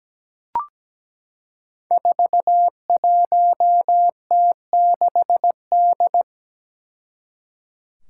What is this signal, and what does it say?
Morse code, heard as a keyed tone, about 17 wpm, 700 Hz. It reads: 41T6D
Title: Five Challenging Letter-Number Combinations 17wpm